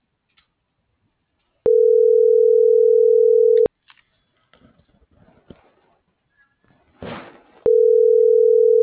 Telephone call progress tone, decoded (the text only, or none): ringback tone